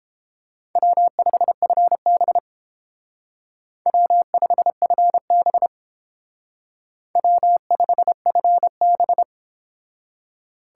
Morse code, heard as a keyed tone, about 30 wpm, 700 Hz. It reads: W5FB W5FB W5FB